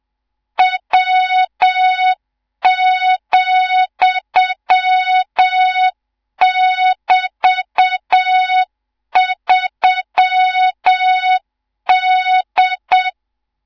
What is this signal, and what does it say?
Morse code, heard as a keyed tone, about 7 wpm, 3700 Hz, with noise. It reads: W,<BT>3D
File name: Morze-HA2MN / hang / 18 IRASJELEK2